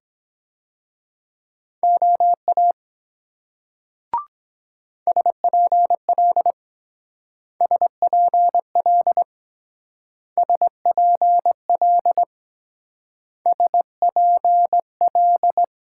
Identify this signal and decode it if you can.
Morse code, heard as a keyed tone, about 24 wpm, 700 Hz. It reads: OA SPL SPL SPL SPL